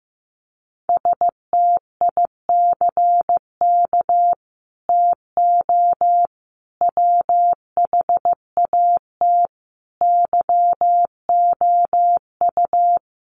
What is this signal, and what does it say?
Morse code, heard as a keyed tone, about 15 wpm, 700 Hz. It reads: STICK TO WHAT YOU
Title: Sentences from Top 1200 Words 15wpm